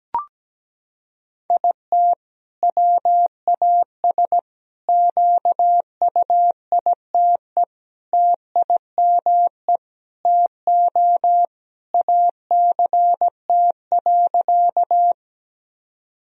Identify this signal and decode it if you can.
Morse code, heard as a keyed tone, about 17 wpm, 700 Hz. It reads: IT WAS QUITE TIME TO ACT.
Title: Sentences from Top 1000 Words - Review Part 1 - 17wpm